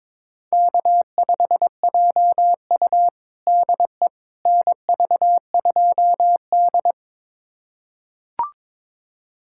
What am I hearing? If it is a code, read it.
Morse code, heard as a keyed tone, about 22 wpm, 700 Hz. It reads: K5JU DE NV2D
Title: USA Callsign DE Callsign - 4 Characters - 22wpm